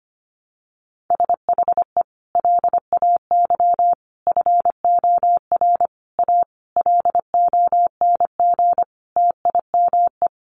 Morse code, heard as a keyed tone, about 25 wpm, 700 Hz. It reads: SHE LAY FOR A LONG TIME
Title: Sentences from Top 900 Words 25wpm